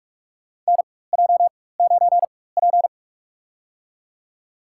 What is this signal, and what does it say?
Morse code, heard as a keyed tone, about 41 wpm, spaced as for 20 wpm, 700 Hz. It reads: NJ9P
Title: USA Call Signs - 4 Characters - 4x Character Spacing - 45wpm